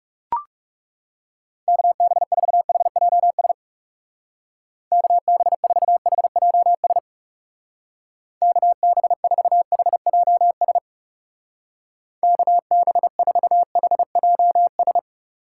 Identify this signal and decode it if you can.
Morse code, heard as a keyed tone, about 38 wpm, 700 Hz. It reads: KB4HJS KB4HJS KB4HJS KB4HJS